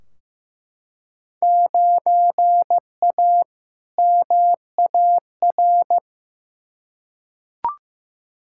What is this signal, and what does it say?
Morse code, heard as a keyed tone, about 15 wpm, 700 Hz. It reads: 9A MAR
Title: Field Day Exchanges Encore - 15wpm